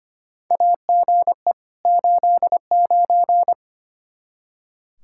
Morse code, heard as a keyed tone, about 25 wpm, 700 Hz. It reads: AGE 89
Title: Practice QSO - Exchange 3 - More Info - 25wpm